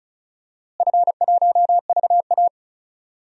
Morse code, heard as a keyed tone, about 36 wpm, 700 Hz. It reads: F1VA
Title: DXCC-100 Call Signs - 4 Characters - Repeated 3x - 35wpm